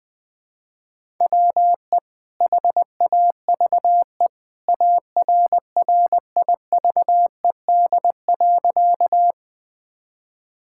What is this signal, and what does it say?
Morse code, heard as a keyed tone, about 20 wpm, 700 Hz. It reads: WE HAVE ARRIVED.
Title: Sentences with 2-Word Phrases - Spoken Beginning - 20wpm